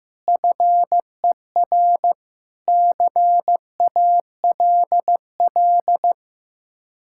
Morse code, heard as a keyed tone, about 15 wpm, 700 Hz. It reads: FER CALL
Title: K1USN SST QSO Practice 15wpm